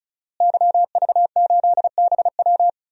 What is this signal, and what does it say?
Morse code, heard as a keyed tone, about 34 wpm, 700 Hz. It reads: YV8BW
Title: DXCC-100 Call Signs - 5 Characters - Repeated 2x - 35wpm